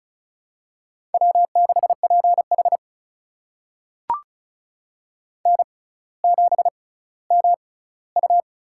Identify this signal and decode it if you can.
Morse code, heard as a keyed tone, about 36 wpm, spaced as for 18 wpm, 700 Hz. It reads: W6PH N7MU
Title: USA Call Signs - 4 Characters - 6x Character Spacing - 35wpm